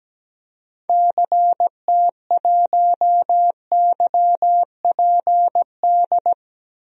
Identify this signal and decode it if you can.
Morse code, heard as a keyed tone, about 17 wpm, 700 Hz. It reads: CT1YPD
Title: DXCC-100 Call Signs - 6 Characters - 17wpm